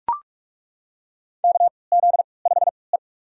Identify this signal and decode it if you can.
Morse code, heard as a keyed tone, about 41 wpm, spaced as for 25 wpm, 700 Hz. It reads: KZ5E
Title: USA Call Signs - 4 Characters - 3x Character Spacing - 45wpm